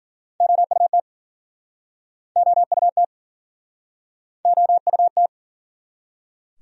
Morse code, heard as a keyed tone, about 52 wpm, 700 Hz. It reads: OUT OUT OUT